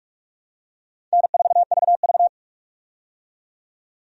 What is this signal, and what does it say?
Morse code, heard as a keyed tone, about 44 wpm, 700 Hz. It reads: N4VV
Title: USA Call Signs - 4 Characters - 45wpm